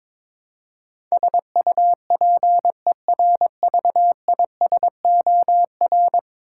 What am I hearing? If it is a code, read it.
Morse code, heard as a keyed tone, about 22 wpm, 700 Hz. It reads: SUPERVISOR
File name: practice / 22wpm / 10 Letter Words - First 3 Letters Given - 22wpm